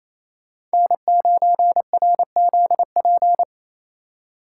Morse code, heard as a keyed tone, about 34 wpm, 700 Hz. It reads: N9RZP
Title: Speed Racing - USA 1x3 Call Signs 28wpm